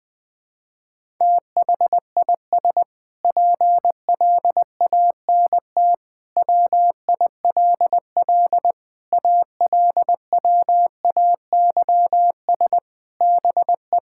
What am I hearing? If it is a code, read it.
Morse code, heard as a keyed tone, about 20 wpm, 700 Hz. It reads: THIS PLANT WILL ALWAYS BE
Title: Sentences from Top 1000 Words - Review Part 6 - 20wpm